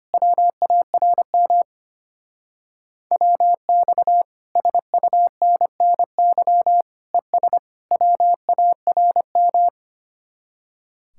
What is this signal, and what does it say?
Morse code, heard as a keyed tone, about 30 wpm, 700 Hz. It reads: WARM WX SUNNY ES WARM
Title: Speed-Racing Common QSO Phrases - 25wpm